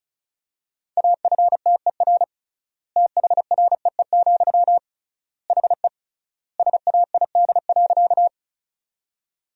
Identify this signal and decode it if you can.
Morse code, heard as a keyed tone, about 36 wpm, 700 Hz. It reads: AFTER THREE, HE SAID.
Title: 3x Word-Spacing - Sentences from Top 200 Words - 35wpm